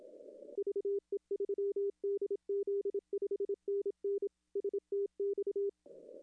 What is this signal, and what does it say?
Morse code, heard as a keyed tone, about 27 wpm, 400 Hz. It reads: VE3DZ 5NN STX
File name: contests / arrl-160-k5tr-op-w5tn -0600